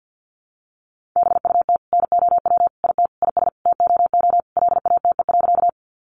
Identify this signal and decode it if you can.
Morse code, heard as a keyed tone, about 48 wpm, 700 Hz. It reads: BUT NOW IT IS TOO LATE.
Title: Sentences from Top 300 Words 50wpm